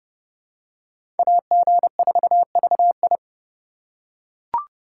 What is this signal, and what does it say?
Morse code, heard as a keyed tone, about 30 wpm, 700 Hz. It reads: AG4VI
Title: USA Call Signs - 5 Characters - 2x Character Spacing - 30wpm